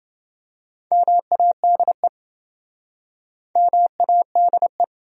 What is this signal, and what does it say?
Morse code, heard as a keyed tone, about 29 wpm, 700 Hz. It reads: MADE MADE